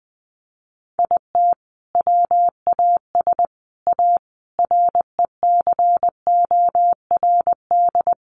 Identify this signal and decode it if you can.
Morse code, heard as a keyed tone, about 20 wpm, 700 Hz. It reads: IT WAS A RECORD
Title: Sentences from Top 500 Words 20wpm